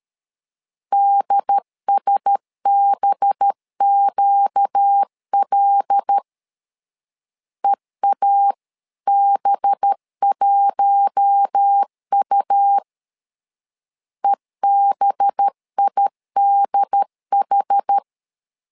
Morse code, heard as a keyed tone, about 13 wpm, 800 Hz. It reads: DSBQL EAB1U EBIDH
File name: LEZIONI / 5 LBVC / 5 LB 12wpm 1906